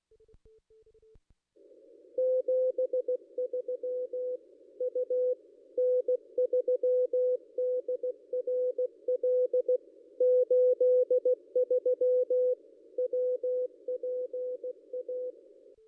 Morse code, heard as a keyed tone, about 16 wpm, 500 Hz. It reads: 73 U N3DRL 83 WPA
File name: contests / ss-cw-k5tr-op-n5rz -1400